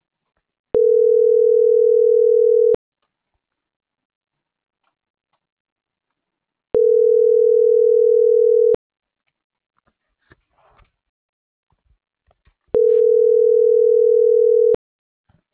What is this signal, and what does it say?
Ringback tone (North America), 440 + 480 Hz.